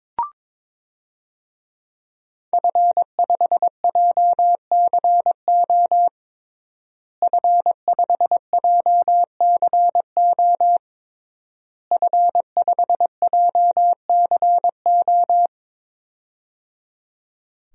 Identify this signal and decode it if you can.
Morse code, heard as a keyed tone, about 22 wpm, 700 Hz. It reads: F5JCO F5JCO F5JCO
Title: DXCC-100 Call Signs - 5 Characters - Repeated 3x - 22wpm